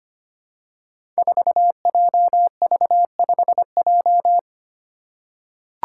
Morse code, heard as a keyed tone, about 25 wpm, 700 Hz. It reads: 4JV5J
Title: Five Challenging Letter-Number Combinations 25wpm